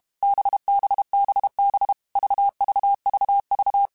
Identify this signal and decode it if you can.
Morse code, heard as a keyed tone, about 32 wpm, 800 Hz. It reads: BBBB VVVV
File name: Groups@30 / Groups@30 mp3s / VB46@30